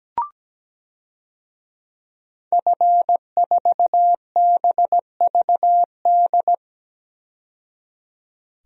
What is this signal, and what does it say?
Morse code, heard as a keyed tone, about 17 wpm, 700 Hz. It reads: F4BVD
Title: DXCC-100 Call Signs - 5 Characters - 17wpm